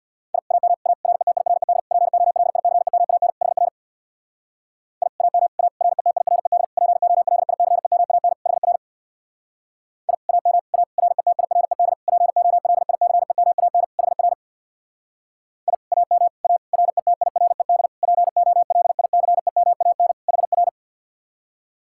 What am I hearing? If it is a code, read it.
Morse code, heard as a keyed tone, about 62 wpm, 700 Hz. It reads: I AM A RETIRED POLICEMAN HR I AM A RETIRED POLICEMAN HR I AM A RETIRED POLICEMAN HR I AM A RETIRED POLICEMAN HR
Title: Speed-Racing Common QSO Phrases Encore - 50wpm